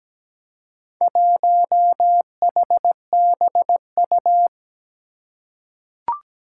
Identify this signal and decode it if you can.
Morse code, heard as a keyed tone, about 17 wpm, 700 Hz. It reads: 1HBU
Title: Four Challenging Letter-Number Combinations 17wpm